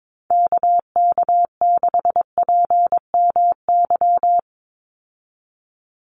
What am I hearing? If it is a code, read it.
Morse code, heard as a keyed tone, about 22 wpm, 700 Hz. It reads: KK6PMY
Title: USA Call Signs 22wpm